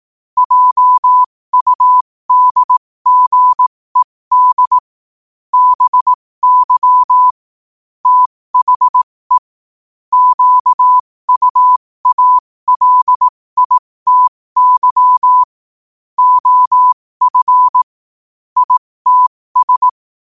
Morse code, heard as a keyed tone, about 18 wpm, spaced as for 15 wpm, 1000 Hz. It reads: JUDGED BY THE QUALITY OF ITS
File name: qotdfiles / 15wpm